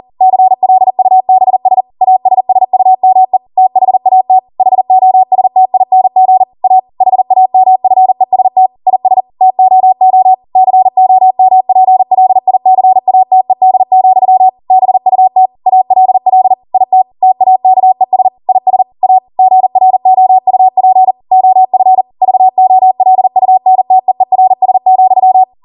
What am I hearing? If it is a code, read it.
Morse code, heard as a keyed tone, about 40 wpm, 750 Hz. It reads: CLUBS ASSUME THAT HOSTING A HAMFEST IS TOO COMPLICATED, BUT ALL IT TAKES IS A GROUP OF VOLUNTEERS,